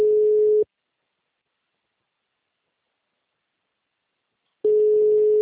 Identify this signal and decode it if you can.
Ringback tone (Europe, 425 Hz).
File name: out- -2015- - - .50386